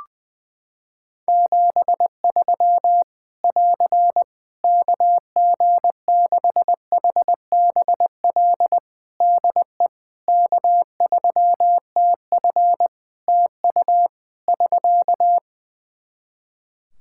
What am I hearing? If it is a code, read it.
Morse code, heard as a keyed tone, about 20 wpm, 700 Hz. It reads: 73 <AR> KG6HBL DE K3TF TU <SK>
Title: Practice QSO - Exchange 4 - Ending - 20wpm